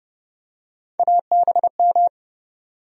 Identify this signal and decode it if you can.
Morse code, heard as a keyed tone, about 30 wpm, 700 Hz. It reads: ABM
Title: USA Call Sign Postfixes - 3 Letters 30wpm